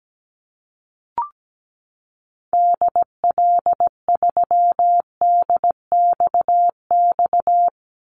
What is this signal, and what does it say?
Morse code, heard as a keyed tone, about 17 wpm, 700 Hz. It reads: DL3DXX
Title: Call Signs Repeated 3x - 17wpm